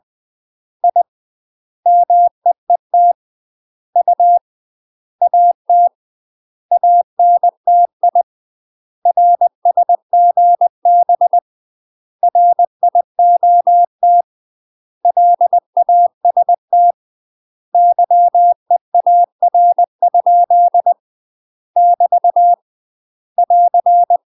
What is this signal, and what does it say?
Morse code, heard as a keyed tone, about 20 wpm, 700 Hz. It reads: I MEET U AT ANTI RSGB RIOT LAST YEAR? <BT> <AR>
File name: UK MASTER QSOs TRAINING / 20WPM / UK MASTER QSOs 0025